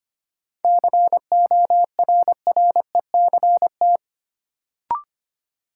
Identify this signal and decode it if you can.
Morse code, heard as a keyed tone, about 25 wpm, 700 Hz. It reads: CORRECT
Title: Sentences from Top 700 Words - New Word Drill - 25wpm